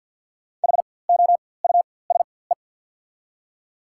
Morse code, heard as a keyed tone, about 53 wpm, spaced as for 24 wpm, 700 Hz. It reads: HOUSE
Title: 4x Character Spacing - Top 300 Words - 50wpm